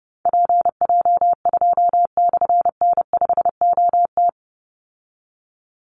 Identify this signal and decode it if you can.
Morse code, heard as a keyed tone, about 30 wpm, 700 Hz. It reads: PJ2/N5OT
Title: Call Signs - 30wpm